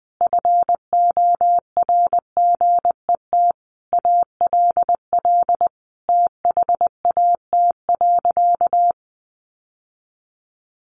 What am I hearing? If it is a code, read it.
Morse code, heard as a keyed tone, about 20 wpm, 700 Hz. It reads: FORGET ALL THAT.
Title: Sentences from Top 1200 Words 20wpm